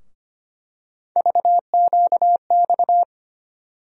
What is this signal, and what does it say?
Morse code, heard as a keyed tone, about 25 wpm, 700 Hz. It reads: VQX
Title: USA Call Sign Postfixes - 3 Letters 25wpm